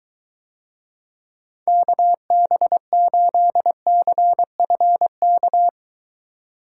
Morse code, heard as a keyed tone, about 23 wpm, 700 Hz. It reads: KB8CFK